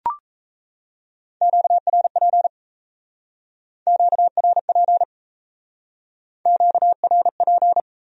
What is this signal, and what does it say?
Morse code, heard as a keyed tone, about 34 wpm, 700 Hz. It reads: QRP QRP QRP
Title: Speed-Racing Q-Codes - 28wpm